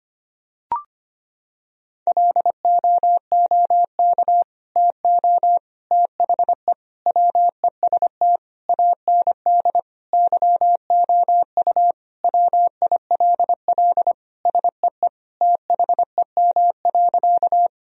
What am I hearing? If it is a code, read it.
Morse code, heard as a keyed tone, about 25 wpm, 700 Hz. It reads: LOOK TO THE WEST AND YOU WILL SEE THEM.
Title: Sentences from Top 1000 Words - Review Part 2 - 25wpm